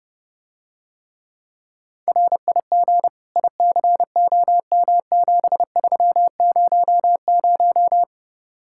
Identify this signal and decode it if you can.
Morse code, heard as a keyed tone, about 32 wpm, 700 Hz. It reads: RIG ICOM7300